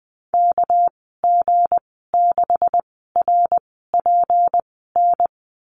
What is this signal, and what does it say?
Morse code, heard as a keyed tone, about 20 wpm, spaced as for 14 wpm, 700 Hz. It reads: KG6RPN
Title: 219 Lesson 35g - Number 0 - Call Signs 20wpm